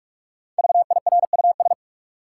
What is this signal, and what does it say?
Morse code, heard as a keyed tone, about 43 wpm, 700 Hz. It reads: VIRUS
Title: Five Letter Words - 45wpm